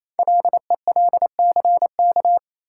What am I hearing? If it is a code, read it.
Morse code, heard as a keyed tone, about 28 wpm, 700 Hz. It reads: LELCK
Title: Code Groups - 5 Letters - 28wpm